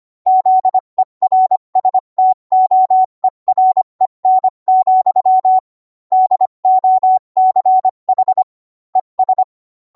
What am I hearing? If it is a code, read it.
Morse code, heard as a keyed tone, about 25 wpm, 750 Hz. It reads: ZERSTOEREN, DOCH ES